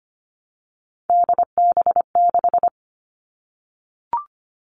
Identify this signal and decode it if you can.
Morse code, heard as a keyed tone, about 25 wpm, 700 Hz. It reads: DB6
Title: All Cut Numbers - 3-Digit 25wpm